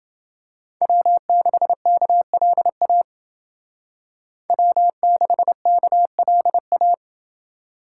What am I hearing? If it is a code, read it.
Morse code, heard as a keyed tone, about 29 wpm, 700 Hz. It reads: W6KLA W6KLA